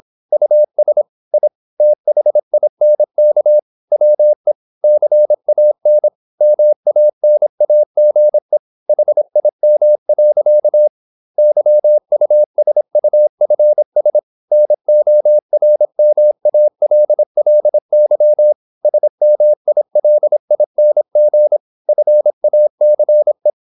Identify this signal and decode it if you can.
Morse code, heard as a keyed tone, about 26 wpm, 600 Hz. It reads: US I THINK WE CAN MANAGE HIM. YUSUFS NORMALLY SMILING FACE
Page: Never 26wpm | CW med Gnister